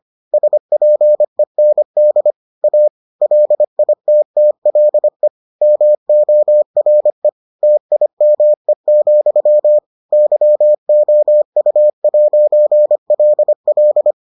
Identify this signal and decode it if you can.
Morse code, heard as a keyed tone, about 25 wpm, 600 Hz. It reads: SPEND A LITTLE MORE TIME, YOU'LL